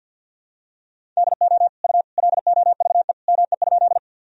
Morse code, heard as a keyed tone, about 51 wpm, 700 Hz. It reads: DO U LOVE ME?
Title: Sentences from Top 400 Words and Abbreviations 50wpm